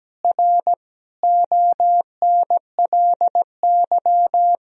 Morse code, heard as a keyed tone, about 17 wpm, 700 Hz. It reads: R ONLY
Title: Sentences from Top 300 Words and Abbreviations 17wpm